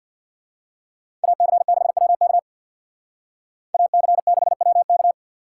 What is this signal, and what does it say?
Morse code, heard as a keyed tone, about 55 wpm, 700 Hz. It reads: AC6WK AC6WK
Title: Speed Racing - USA 2x2 Call Signs 50wpm